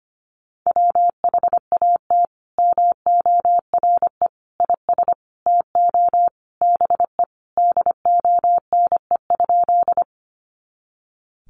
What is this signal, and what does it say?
Morse code, heard as a keyed tone, about 25 wpm, 700 Hz. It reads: WHAT MORE IS TO BE DONE?
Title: Sentences from Top 500 Words 25wpm